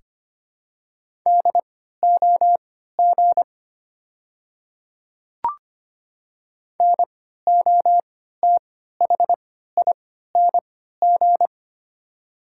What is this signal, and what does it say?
Morse code, heard as a keyed tone, about 25 wpm, spaced as for 14 wpm, 700 Hz. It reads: DOG NOTHING
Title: 3x Character Spacing - Top 500 Words - 25wpm